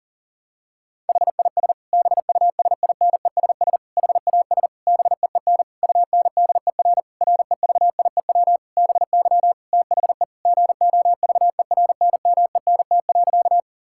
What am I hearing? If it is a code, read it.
Morse code, heard as a keyed tone, about 40 wpm, 700 Hz. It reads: HIS BUSINESS HAS BEEN UNDER REVIEW BY THE GOVERNMENT.
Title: Sentences from Top 1000 Words - Review Part 6 - 40wpm